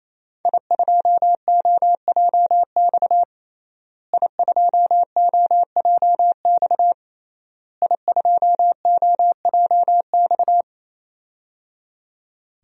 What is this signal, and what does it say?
Morse code, heard as a keyed tone, about 28 wpm, 700 Hz. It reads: I2OJX I2OJX I2OJX
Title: DXCC-100 Call Signs - 5 Characters - Repeated 3x - 28wpm